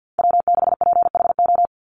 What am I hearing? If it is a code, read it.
Morse code, heard as a keyed tone, about 50 wpm, 700 Hz. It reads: W6PHO
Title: Call Signs Repeated 3x - 50wpm